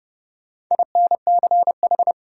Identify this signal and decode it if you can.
Morse code, heard as a keyed tone, about 32 wpm, 700 Hz. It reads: INCH
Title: Sentences from Top 1000 Words - Review Part 3 - New Word Drill - 30wpm